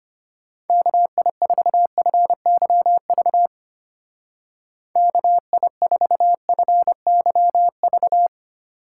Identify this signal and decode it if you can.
Morse code, heard as a keyed tone, about 30 wpm, 700 Hz. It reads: KI4FYV KI4FYV
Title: Speed Racing - USA Call Signs 25wpm